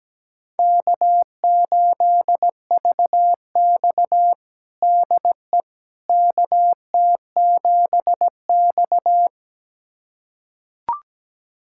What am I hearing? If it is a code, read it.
Morse code, heard as a keyed tone, about 17 wpm, 700 Hz. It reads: K8VX DE KT7X
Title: USA Callsign DE Callsign - 4 Characters - 17wpm